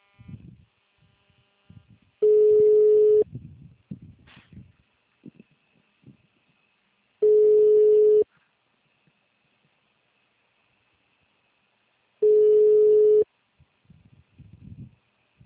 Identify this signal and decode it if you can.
Ringback tone (Europe, 425 Hz).